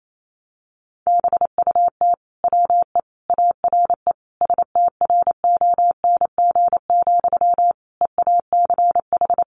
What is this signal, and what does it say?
Morse code, heard as a keyed tone, about 28 wpm, 700 Hz. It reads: BUT WE ARE STRONG, EACH
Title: Sentences from Top 800 Words 28wpm